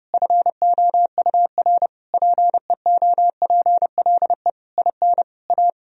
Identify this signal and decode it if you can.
Morse code, heard as a keyed tone, about 30 wpm, 700 Hz. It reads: FOUR PEOPLE IN A
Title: Sentences from Top 1000 Words - Review Part 3 - 30wpm